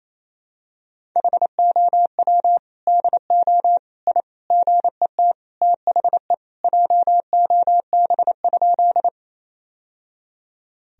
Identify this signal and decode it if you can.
Morse code, heard as a keyed tone, about 28 wpm, 700 Hz. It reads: HOW DO I GET THE JOB?
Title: Sentences from Top 1000 Words - Review Part 3 - 28wpm